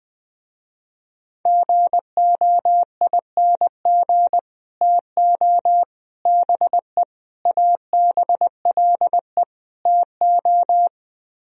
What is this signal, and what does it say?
Morse code, heard as a keyed tone, about 21 wpm, 700 Hz. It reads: GOING TO BE ABLE TO
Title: Speed-Racing - Sets of 5 Words 20wpm